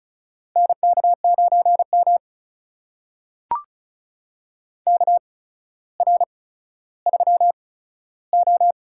Morse code, heard as a keyed tone, about 36 wpm, spaced as for 15 wpm, 700 Hz. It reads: NK9M KR3O
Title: USA Call Signs - 4 Characters - 8x Character Spacing - 35wpm